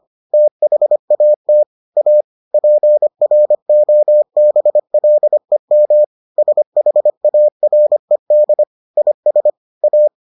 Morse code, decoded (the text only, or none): THAT A PROBLEM SHARED IS A